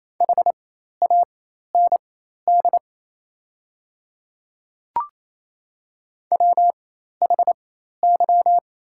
Morse code, heard as a keyed tone, about 27 wpm, spaced as for 13 wpm, 690 Hz. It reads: HAND WHY
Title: 4x Character Spacing - Top 200 Words - 28wpm